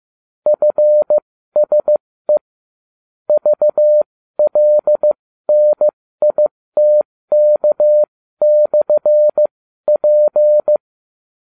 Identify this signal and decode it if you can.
Morse code, heard as a keyed tone, about 15 wpm, spaced as for 13 wpm, 600 Hz. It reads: FSE VLNITK/P